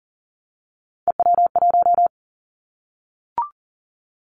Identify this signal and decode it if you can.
Morse code, heard as a keyed tone, about 42 wpm, 700 Hz. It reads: EW1
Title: Callsign Prefixes for Top 50 DXCC - 40wpm